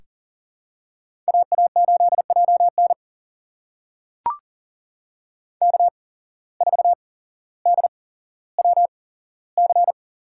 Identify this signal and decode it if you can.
Morse code, heard as a keyed tone, about 39 wpm, spaced as for 17 wpm, 700 Hz. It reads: AA8JN K4DWC
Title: USA Call Signs - 5 Characters - 8x Character Spacing - 40wpm